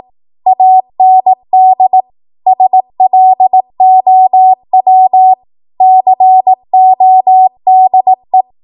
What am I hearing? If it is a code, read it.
Morse code, heard as a keyed tone, about 18 wpm, 750 Hz. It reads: AND SLOW CODE
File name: Archive / 18 WPM / 18WPM